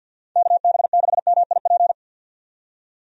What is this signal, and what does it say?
Morse code, heard as a keyed tone, about 48 wpm, 700 Hz. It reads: KB6MIP
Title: USA Call Signs - 6 Characters - 50wpm